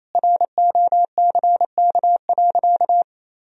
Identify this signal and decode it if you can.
Morse code, heard as a keyed tone, about 28 wpm, 700 Hz. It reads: ROCK.
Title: Sentences from Top 1000 Words - Review Part 2 - 28wpm